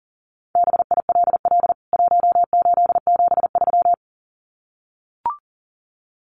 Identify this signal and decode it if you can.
Morse code, heard as a keyed tone, about 39 wpm, 700 Hz. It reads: BILL 1873
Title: CWT Exchange 40wpm